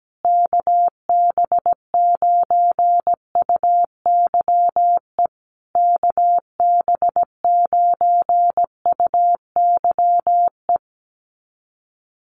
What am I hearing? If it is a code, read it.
Morse code, heard as a keyed tone, about 17 wpm, 700 Hz. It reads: KB9UYE KB9UYE
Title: USA Call Signs Repeated 3x 17wpm